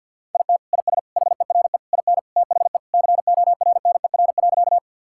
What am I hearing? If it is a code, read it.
Morse code, heard as a keyed tone, about 50 wpm, 700 Hz. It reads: IT IS HERE IN THE CORNER.
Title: Sentences from Top 1600 Words 50wpm